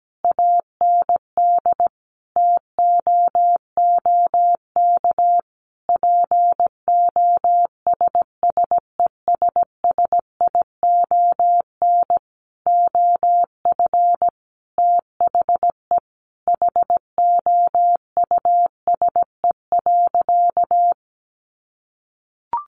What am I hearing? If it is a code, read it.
Morse code, heard as a keyed tone, about 17 wpm, 700 Hz. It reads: AND TOOK POSSESSION OF THE HOUSE.
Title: Sentences from Top 1400 Words 17wpm